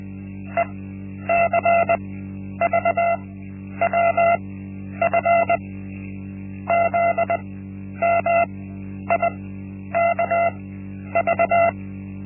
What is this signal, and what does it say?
Morse code, heard as a keyed tone, about 20 wpm, spaced as for 10 wpm, 680 Hz. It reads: ECVWF ZMIKV
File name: Linksmakalnis-1